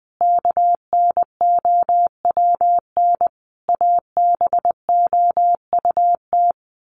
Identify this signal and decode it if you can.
Morse code, heard as a keyed tone, about 19 wpm, 700 Hz. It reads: KNOWN ABOUT
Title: Sentences from Top 1200 Words 20wpm